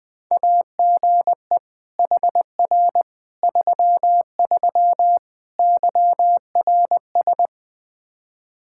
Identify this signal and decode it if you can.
Morse code, heard as a keyed tone, about 20 wpm, 700 Hz. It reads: AGE HR 33 YRS
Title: K1USN SST QSO Practice 20wpm